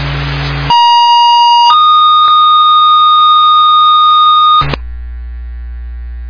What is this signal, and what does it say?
Two-tone page, heard as a one-second tone, then a three-second tone, about 950 then 1230 Hz. It is heard from 0.7 to 4.6 s.
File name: Long Lake Fire 2021 08 12 14 49 16